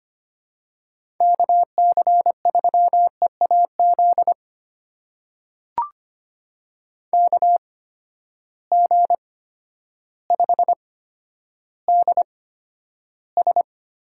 Morse code, heard as a keyed tone, about 25 wpm, spaced as for 11 wpm, 700 Hz. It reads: KC3EAZ KG5DS
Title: USA Call Signs - 6 Characters - 8x Character Spacing - 25wpm